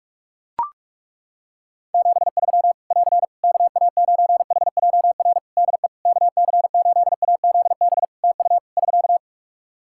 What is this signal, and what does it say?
Morse code, heard as a keyed tone, about 41 wpm, 700 Hz. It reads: 73 <AR> KA9HJR DE KC8AZB TU <SK>
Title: Practice QSO - Exchange 4 - Ending - 45wpm